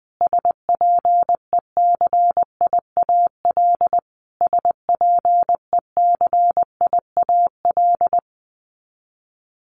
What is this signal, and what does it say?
Morse code, heard as a keyed tone, about 20 wpm, 700 Hz. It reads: SPECIAL SPECIAL
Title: Top 500 Words Repeated 20wpm